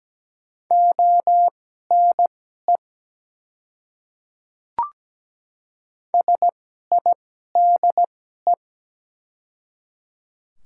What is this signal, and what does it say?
Morse code, heard as a keyed tone, about 17 wpm, spaced as for 12 wpm, 690 Hz. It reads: ONE SIDE
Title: 2x Character Spacing - Top 100 Words - 17wpm